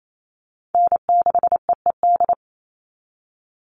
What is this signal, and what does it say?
Morse code, heard as a keyed tone, about 28 wpm, 700 Hz. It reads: N6EED
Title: Call Signs - 28wpm